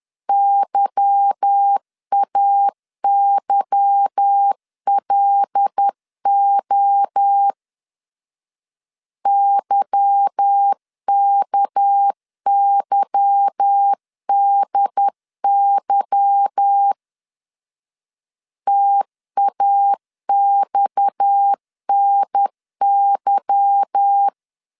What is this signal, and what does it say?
Morse code, heard as a keyed tone, about 10 wpm, 800 Hz. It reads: YAYLO YKYDY TAXNY
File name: LEZIONI / 8 YJ45 / 8 Y 10wpm 2615